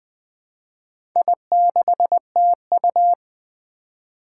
Morse code, heard as a keyed tone, about 21 wpm, 700 Hz. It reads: I6TU
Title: Four Challenging Letter-Number Combinations 20wpm